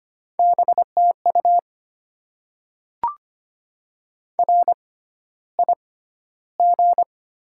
Morse code, heard as a keyed tone, about 25 wpm, spaced as for 13 wpm, 700 Hz. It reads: BTU RIG
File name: Top 20 QSO Elements - 6x Character Spacing - 25wpm